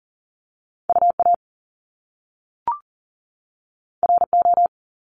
Morse code, heard as a keyed tone, about 38 wpm, 700 Hz. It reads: UA RO
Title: Top Two Letter Combinations in Words 40wpm